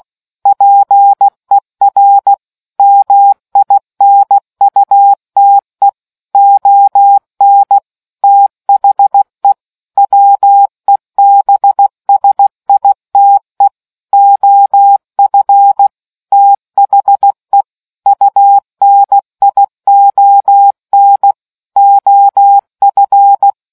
Morse code, heard as a keyed tone, about 16 wpm, 800 Hz. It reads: PER MINUTE ON THE WEBSITE OF THE UNION OF